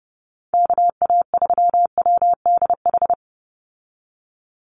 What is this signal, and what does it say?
Morse code, heard as a keyed tone, about 29 wpm, 700 Hz. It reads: KA3WDH
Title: USA Call Signs 30wpm